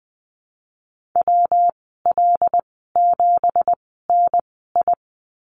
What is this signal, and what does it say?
Morse code, heard as a keyed tone, about 20 wpm, spaced as for 15 wpm, 700 Hz. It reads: WL7NI
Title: 173 Lesson 28g - Number 9 - Call Signs 20wpm